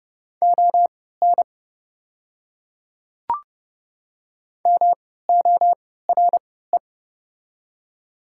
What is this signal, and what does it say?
Morse code, heard as a keyed tone, about 29 wpm, spaced as for 17 wpm, 690 Hz. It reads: ON MORE
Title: 3x Character Spacing - Top 100 Words - 30wpm